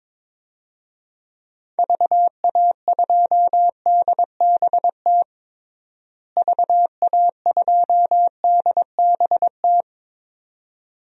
Morse code, heard as a keyed tone, about 22 wpm, 700 Hz. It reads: VA2DBT VA2DBT
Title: DXCC-100 Call Signs - 6 Characters - Repeated 2x - 22wpm